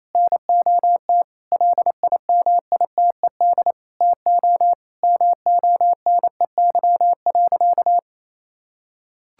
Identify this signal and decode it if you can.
Morse code, heard as a keyed tone, about 28 wpm, 700 Hz. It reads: NOT LIMITED TO MONEY.
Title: Sentences from Top 1000 Words - Review Part 6 - 28wpm